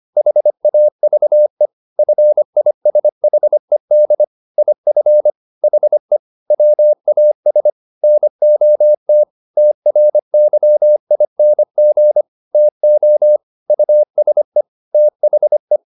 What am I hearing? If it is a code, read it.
Morse code, heard as a keyed tone, about 25 wpm, 600 Hz. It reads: HAVE FISHED IF HE WAS NOT TRYING TO USE THE
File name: the old man and the sea part 1 25wpm